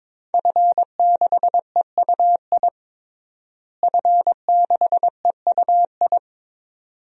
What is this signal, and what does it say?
Morse code, heard as a keyed tone, about 22 wpm, 700 Hz. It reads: F6EUI F6EUI
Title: DXCC-100 Call Signs - 5 Characters - Repeated 2x - 22wpm